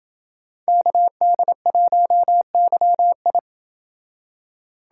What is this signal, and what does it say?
Morse code, heard as a keyed tone, about 27 wpm, 700 Hz. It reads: KD1YI